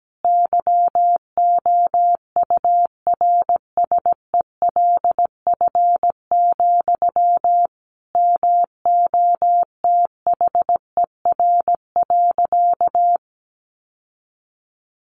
Morse code, heard as a keyed tone, about 17 wpm, 700 Hz. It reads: YOURSELF, MOTHER.
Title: Sentences from Top 1300 Words 17wpm